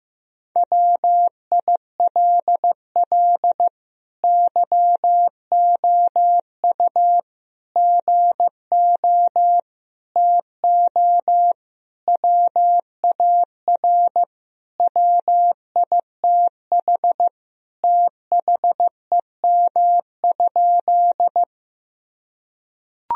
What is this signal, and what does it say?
Morse code, heard as a keyed tone, about 15 wpm, 700 Hz. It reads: WILL YOU GO TO WAR WITH THEM?
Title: Sentences from Top 1000 Words - Review Part 2 - 15wpm